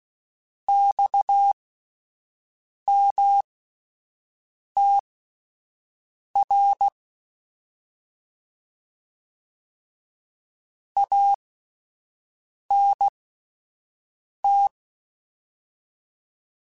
Morse code, heard as a keyed tone, about 16 wpm, spaced as for 5 wpm, 800 Hz. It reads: XMTR ANT
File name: Abreviations / Abreviations mp3s / Abreviations@5